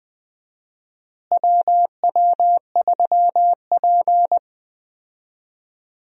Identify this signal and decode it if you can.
Morse code, heard as a keyed tone, about 20 wpm, 700 Hz. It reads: WW3P
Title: USA Call Signs - 4 Characters - Repeated 3x - 20wpm